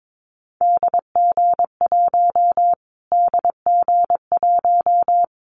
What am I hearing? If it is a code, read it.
Morse code, heard as a keyed tone, about 22 wpm, 700 Hz. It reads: DG1 DG1
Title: Callsign Prefixes for Top 25 DXCC - Repeated 3x - 22wpm